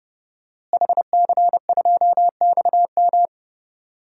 Morse code, heard as a keyed tone, about 30 wpm, 700 Hz. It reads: HC2XM
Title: DXCC-100 Call Signs - 5 Characters - Repeated 2x - 30wpm